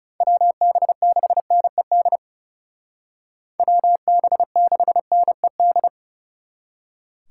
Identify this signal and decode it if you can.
Morse code, heard as a keyed tone, about 37 wpm, 700 Hz. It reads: WB6NED WB6NED
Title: Speed Racing - USA 2x3 Call Signs 30wpm